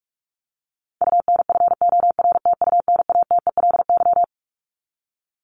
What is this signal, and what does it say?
Morse code, heard as a keyed tone, about 44 wpm, 700 Hz. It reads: UNFORTUNATELY
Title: LY Suffix - Top 100 Words with Base Words 45wpm